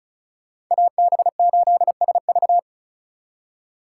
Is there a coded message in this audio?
Yes: Morse code from 0.7 to 2.6 s, heard as a keyed tone, about 44 wpm, 700 Hz. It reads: AB8SV